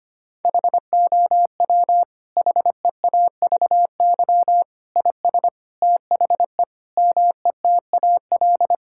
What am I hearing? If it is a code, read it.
Morse code, heard as a keyed tone, about 25 wpm, 700 Hz. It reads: HOW HEAVY IS THE METAL
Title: Sentences from Top 1000 Words - Review Part 6 - 25wpm